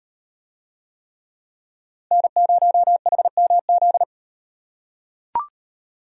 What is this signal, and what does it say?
Morse code, heard as a keyed tone, about 38 wpm, 700 Hz. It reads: N0HMZ